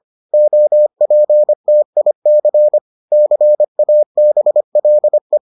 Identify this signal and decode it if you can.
Morse code, heard as a keyed tone, about 25 wpm, 600 Hz. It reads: OPTIC CABLE